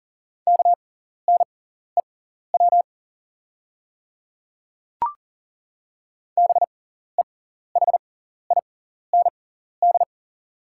Morse code, heard as a keyed tone, about 40 wpm, spaced as for 14 wpm, 690 Hz. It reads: KNEW BEHIND
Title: 6x Character Spacing - Top 500 Words - 40wpm